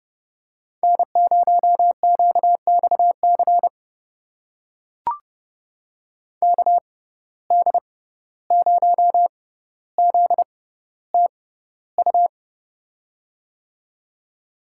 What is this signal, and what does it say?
Morse code, heard as a keyed tone, about 30 wpm, spaced as for 15 wpm, 700 Hz. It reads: N0QXC KD0ZTU
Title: USA Call Signs - 6x Character Spacing - 30wpm